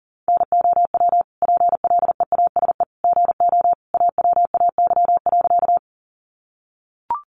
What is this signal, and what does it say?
Morse code, heard as a keyed tone, about 40 wpm, 700 Hz. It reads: NOW PLEASE GO AWAY.
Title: Sentences from Top 1500 Words 40wpm